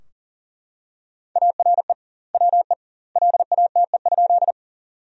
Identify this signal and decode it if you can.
Morse code, heard as a keyed tone, about 40 wpm, 700 Hz. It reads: ARE WE LATE?
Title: 2x Word-Spacing - Sentences from Top 300 Words - 40wpm